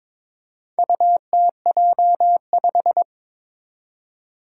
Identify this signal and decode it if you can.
Morse code, heard as a keyed tone, about 22 wpm, 700 Hz. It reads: UTJ5
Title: Four Challenging Letter-Number Combinations 22wpm